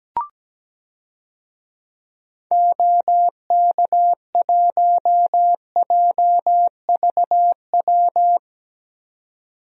Morse code, heard as a keyed tone, about 17 wpm, 700 Hz. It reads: OK1JVW
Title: DXCC-100 Call Signs - 6 Characters - 17wpm